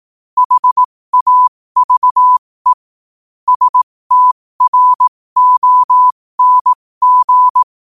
Morse code, heard as a keyed tone, about 18 wpm, spaced as for 15 wpm, 1000 Hz. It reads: HAVE STRONG
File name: qotdfiles / 15wpm